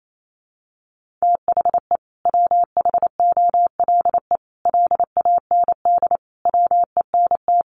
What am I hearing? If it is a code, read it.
Morse code, heard as a keyed tone, about 28 wpm, 700 Hz. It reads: THE WHOLE LAND WENT
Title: Sentences from Top 1400 Words 28wpm